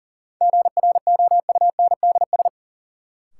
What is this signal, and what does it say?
Morse code, heard as a keyed tone, about 40 wpm, 700 Hz. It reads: GROUNDS
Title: Sentences from Top 1300 Words - New Word Drill Alt - 40wpm